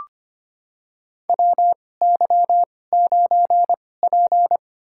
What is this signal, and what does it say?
Morse code, heard as a keyed tone, about 25 wpm, spaced as for 18 wpm, 700 Hz. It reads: WY9P
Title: USA Call Signs - 4 Characters - 2x Character Spacing - 25wpm